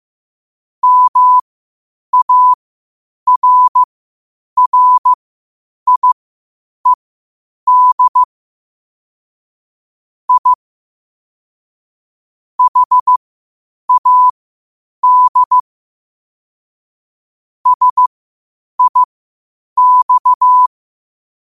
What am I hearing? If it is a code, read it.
Morse code, heard as a keyed tone, about 15 wpm, spaced as for 8 wpm, 1000 Hz. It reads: MARRIED I HAD SIX
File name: qotdfiles / 8wpm